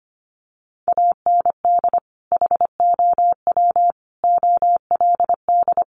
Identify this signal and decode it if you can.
Morse code, heard as a keyed tone, about 25 wpm, 700 Hz. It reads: AND HOW OLD
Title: Sentences from Top 200 Words 25wpm